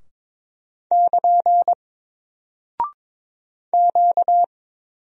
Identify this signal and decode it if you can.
Morse code, heard as a keyed tone, about 22 wpm, 700 Hz. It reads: <KN> Q
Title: Single Character (Letter:Number:Punctuation:Prosign) Alt 22wpm